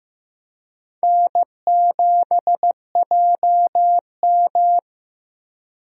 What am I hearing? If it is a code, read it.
Morse code, heard as a keyed tone, about 15 wpm, 700 Hz. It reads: N7JM
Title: USA Call Signs - 4 Characters - Repeated 3x - 15wpm